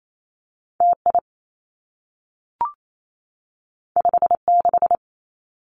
Morse code, heard as a keyed tone, about 28 wpm, 700 Hz. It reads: T I 5 6
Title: Two Challenging Letter-Number Combinations 28wpm